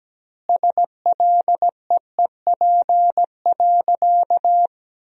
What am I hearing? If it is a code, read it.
Morse code, heard as a keyed tone, about 17 wpm, 700 Hz. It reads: SLEEP.
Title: 2x Word-Spacing - Sentences from Top 500 Words - 17wpm